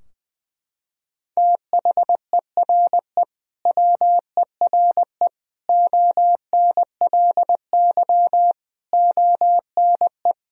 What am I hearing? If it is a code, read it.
Morse code, heard as a keyed tone, about 20 wpm, 700 Hz. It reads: THERE WERE ONLY ONE
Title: Sentences from Top 1000 Words - Review Part 2 - 20wpm